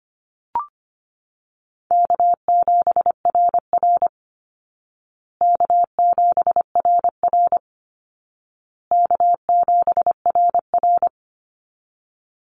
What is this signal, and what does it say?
Morse code, heard as a keyed tone, about 25 wpm, 700 Hz. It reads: K7RR K7RR K7RR
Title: Easy Call Signs Repeated 3x - 25wpm